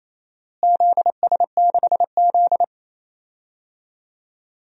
Morse code, heard as a keyed tone, about 28 wpm, 700 Hz. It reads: ZS6Z
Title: DXCC-100 Call Signs - 4 Characters - Repeated 2x - 28wpm